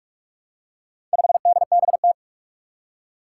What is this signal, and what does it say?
Morse code, heard as a keyed tone, about 46 wpm, 700 Hz. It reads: 5DBT
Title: Four Challenging Letter-Number Combinations 45wpm